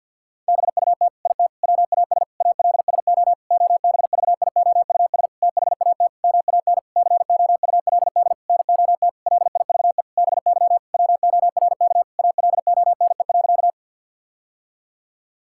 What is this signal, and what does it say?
Morse code, heard as a keyed tone, about 50 wpm, 700 Hz. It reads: BUT IT WAS ALSO OBVIOUS THAT MAN COULD NOT LIVE BY WORK ALONE.
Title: Sentences from Top 1600 Words 50wpm